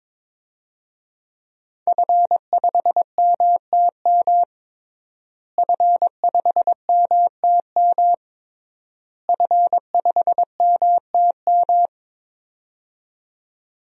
Morse code, heard as a keyed tone, about 22 wpm, 700 Hz. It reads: F5MTM F5MTM F5MTM
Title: DXCC-100 Call Signs - 5 Characters - Repeated 3x - 22wpm